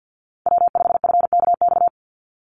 Morse code, heard as a keyed tone, about 50 wpm, 700 Hz. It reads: W5FKX
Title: Call Signs - 50wpm